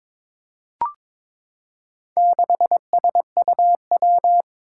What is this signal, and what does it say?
Morse code, heard as a keyed tone, about 22 wpm, 700 Hz. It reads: 6SUW
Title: Four Challenging Letter-Number Combinations 22wpm